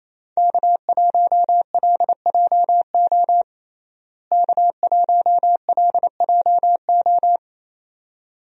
Morse code, heard as a keyed tone, about 28 wpm, 700 Hz. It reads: K1LJO K1LJO
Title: USA Call Signs - 5 Characters - Repeated 2x - 28wpm